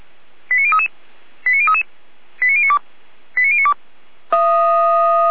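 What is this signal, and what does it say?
Five-tone selective call (ZVEI), heard as quick runs of five tones, tones of about 70 ms. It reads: 89033 89033 89092 89092
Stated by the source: ZVEI five-tone call 89115 received by SDR.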